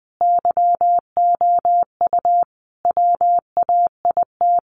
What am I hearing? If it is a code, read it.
Morse code, heard as a keyed tone, about 20 wpm, 700 Hz. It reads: YOU WAIT
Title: Sentences from Top 500 Words 20wpm